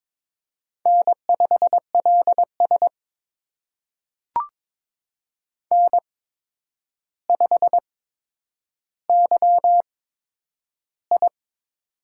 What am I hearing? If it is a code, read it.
Morse code, heard as a keyed tone, about 22 wpm, spaced as for 9 wpm, 700 Hz. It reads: N5LS N5YI
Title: USA Call Signs - 4 Characters - 8x Character Spacing - 22wpm